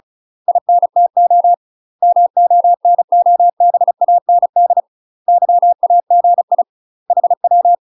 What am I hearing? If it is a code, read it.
Morse code, heard as a keyed tone, about 34 wpm, 700 Hz. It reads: INTO MONOBAND YAGI HW
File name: UK MASTER QSOs TRAINING / 35WPM / UK MASTER QSOs 0004